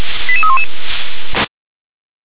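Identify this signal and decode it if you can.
Five-tone selective call (ZVEI), heard as quick runs of five tones, tones of about 70 ms. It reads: 88311
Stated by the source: Five-tone ZVEI selective call received by SDR.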